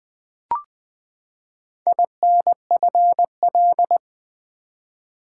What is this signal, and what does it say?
Morse code, heard as a keyed tone, about 20 wpm, 700 Hz. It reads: INFL
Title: Build A Word - 4 Letters At A Time - 11-Letter Words - 20wpm